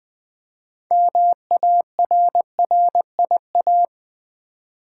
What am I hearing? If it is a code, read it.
Morse code, heard as a keyed tone, about 20 wpm, 700 Hz. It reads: MARRIA
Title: Build A Word - 3 Letters At A Time - 8-Letter Words - 20wpm